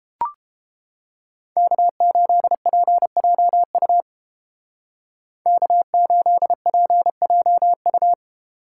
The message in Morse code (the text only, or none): K8PJU K8PJU